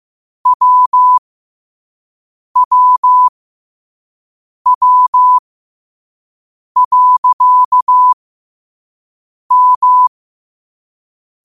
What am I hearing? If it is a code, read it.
Morse code, heard as a keyed tone, about 15 wpm, spaced as for 5 wpm, 1000 Hz. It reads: WWW.M
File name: qotdfiles / 5wpm